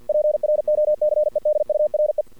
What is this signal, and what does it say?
Morse code, heard as a keyed tone, about 52 wpm, 600 Hz. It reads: PROCEDURE
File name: archived / CRSnet